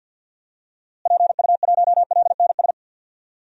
Morse code, heard as a keyed tone, about 50 wpm, 700 Hz. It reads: PU1LNS
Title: DXCC-100 Call Signs - 6 Characters - Repeated 3x - 50wpm